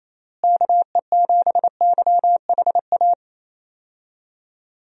Morse code, heard as a keyed tone, about 28 wpm, 700 Hz. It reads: KE7YHA
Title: USA Call Signs - 6 Characters - Repeated 3x - 28wpm